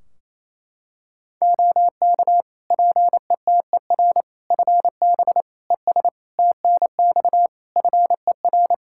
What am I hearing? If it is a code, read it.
Morse code, heard as a keyed tone, about 28 wpm, 700 Hz. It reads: OK PETER FB ES TNX FER
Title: K1USN SST QSO Practice 28wpm